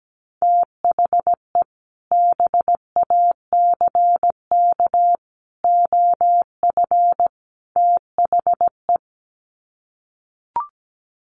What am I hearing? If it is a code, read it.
Morse code, heard as a keyed tone, about 17 wpm, 700 Hz. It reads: THE BACK OF THE
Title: Sets of 4 Words 17wpm